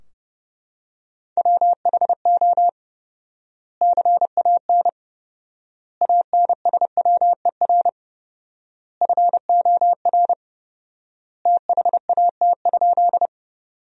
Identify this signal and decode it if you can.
Morse code, heard as a keyed tone, about 30 wpm, 700 Hz. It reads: WHO CAN ANSWER FOR THAT?
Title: 4x Word-Spacing - Sentences from Top 300 Words - 30wpm